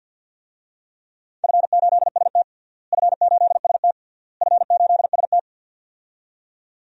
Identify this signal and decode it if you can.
Morse code, heard as a keyed tone, about 46 wpm, 700 Hz. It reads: F8ST F8ST F8ST
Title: DXCC-100 Call Signs - 4 Characters - Repeated 3x - 50wpm